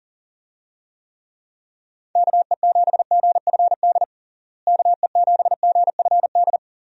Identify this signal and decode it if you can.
Morse code, heard as a keyed tone, about 40 wpm, 700 Hz. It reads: KE7GFD KE7GFD
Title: USA Call Signs - 6 Characters - Repeated 3x - 40wpm